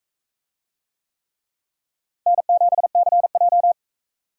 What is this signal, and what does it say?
Morse code, heard as a keyed tone, about 37 wpm, 700 Hz. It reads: N7CJ